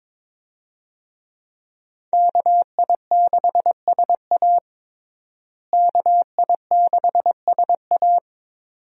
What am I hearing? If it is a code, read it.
Morse code, heard as a keyed tone, about 21 wpm, 700 Hz. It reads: KI6SA KI6SA
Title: USA Call Signs - 5 Characters - Repeated 2x - 22wpm